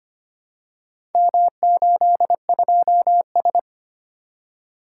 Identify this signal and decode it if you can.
Morse code, heard as a keyed tone, about 25 wpm, 700 Hz. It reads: M82S
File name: practice / 25wpm / Four Letter-Number Combinations - 25wpm